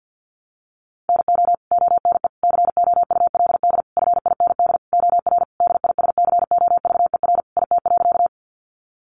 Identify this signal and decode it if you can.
Morse code, heard as a keyed tone, about 50 wpm, 700 Hz. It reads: NO ONE COULD FIND OR DISCOVER IT.
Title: Sentences from Top 1100 Words 50wpm